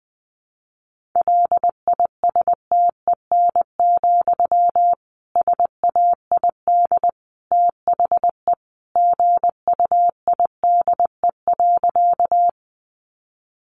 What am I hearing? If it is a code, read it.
Morse code, heard as a keyed tone, about 20 wpm, 700 Hz. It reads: LISTEN, SAID THE GUIDE.
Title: Sentences from Top 600 Words 20wpm